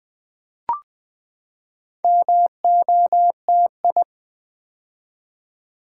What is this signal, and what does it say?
Morse code, heard as a keyed tone, about 20 wpm, 700 Hz. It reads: MOTI
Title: Build A Word - 4 Letters At A Time - 8-Letter Words - 20wpm